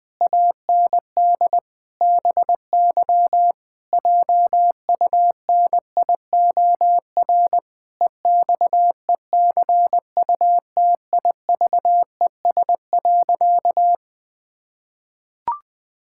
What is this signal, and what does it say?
Morse code, heard as a keyed tone, about 20 wpm, 700 Hz. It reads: AND BY JUNIOR EXECUTIVES.
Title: Pangram Challenge v2 - 20wpm